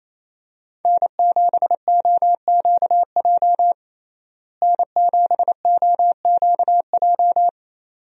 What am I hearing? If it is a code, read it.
Morse code, heard as a keyed tone, about 28 wpm, 700 Hz. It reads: N7OQJ N7OQJ
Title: USA Call Signs - 5 Characters - Repeated 3x - 28wpm